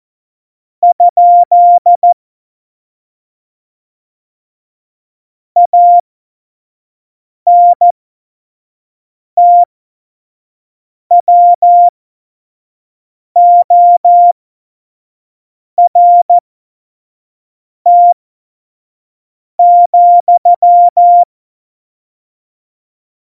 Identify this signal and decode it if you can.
Morse code, heard as a keyed tone, about 14 wpm, spaced as for 5 wpm, 700 Hz. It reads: ? ANTWORT,
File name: Morsekurs / Witze / Witz 09. Oktober 2018, 25 F70